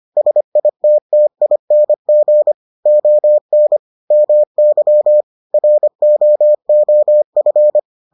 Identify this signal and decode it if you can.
Morse code, heard as a keyed tone, about 25 wpm, 600 Hz. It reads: SITTING ON MY ROOF